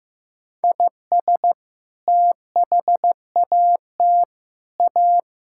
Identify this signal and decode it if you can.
Morse code, heard as a keyed tone, about 15 wpm, 700 Hz. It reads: IS THAT A
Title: Sentences from Top 1000 Words - Review Part 5 - 15wpm